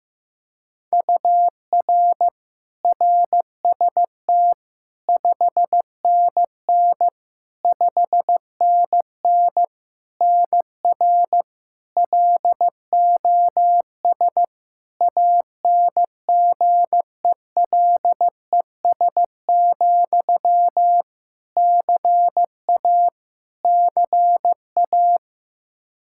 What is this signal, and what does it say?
Morse code, heard as a keyed tone, about 15 wpm, 700 Hz. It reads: UR RST 5NN 5NN NR LOS ANGELES, CA CA
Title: K1USN SST QSO Practice 15wpm